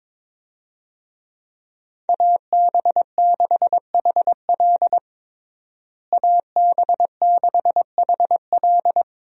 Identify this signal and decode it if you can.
Morse code, heard as a keyed tone, about 22 wpm, 700 Hz. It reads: AB6HL AB6HL
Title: USA Call Signs - 5 Characters - Repeated 3x - 22wpm